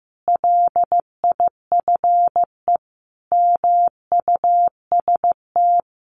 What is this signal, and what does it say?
Morse code, heard as a keyed tone, about 15 wpm, 700 Hz. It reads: LIFE MUST
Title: Sentences from Top 1400 Words 15wpm